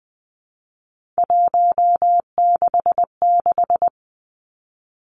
Morse code, heard as a keyed tone, about 20 wpm, 700 Hz. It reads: 166
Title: Three Challenging Letter-Number Combinations 20wpm